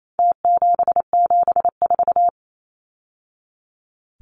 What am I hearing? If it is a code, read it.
Morse code, heard as a keyed tone, about 28 wpm, 700 Hz. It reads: T774
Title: Numbers - Contest Style - 4-Digit 28wpm